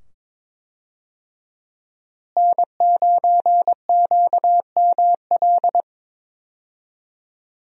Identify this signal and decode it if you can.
Morse code, heard as a keyed tone, about 22 wpm, 700 Hz. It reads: N9QML
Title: USA Call Signs - 5 Characters - Repeated 2x - 22wpm